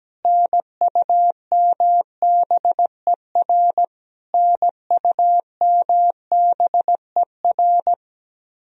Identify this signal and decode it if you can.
Morse code, heard as a keyed tone, about 17 wpm, 700 Hz. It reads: NUMBER NUMBER
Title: Top 100 Words Repeated 17wpm